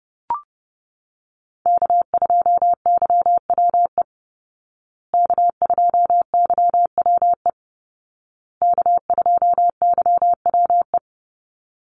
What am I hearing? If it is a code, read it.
Morse code, heard as a keyed tone, about 30 wpm, 700 Hz. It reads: K2YWE K2YWE K2YWE
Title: Call Signs Repeated 2x - 30wpm